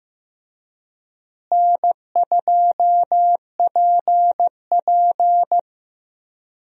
Morse code, heard as a keyed tone, about 16 wpm, 700 Hz. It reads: N2PP
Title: USA Call Signs - 4 Characters - Repeated 2x - 15wpm